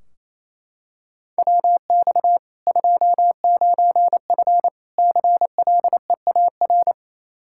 Morse code, heard as a keyed tone, about 28 wpm, 700 Hz. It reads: WX 29F CLEAR
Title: Practice QSO - Exchange 2 - Info - 28wpm